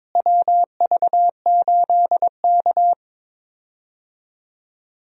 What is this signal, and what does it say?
Morse code, heard as a keyed tone, about 22 wpm, 700 Hz. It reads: WV8K
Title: USA Call Signs - 4 Characters - Repeated 3x - 22wpm